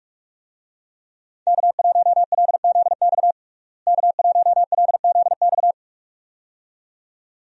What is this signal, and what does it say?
Morse code, heard as a keyed tone, about 42 wpm, 700 Hz. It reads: K1LZX K1LZX
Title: USA Call Signs - 5 Characters - Repeated 2x - 45wpm